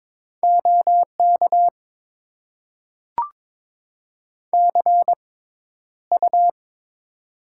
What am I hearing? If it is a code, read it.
Morse code, heard as a keyed tone, about 21 wpm, spaced as for 11 wpm, 700 Hz. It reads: OK CU
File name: Top 90 QSO Elements - 6x Character Spacing - 22wpm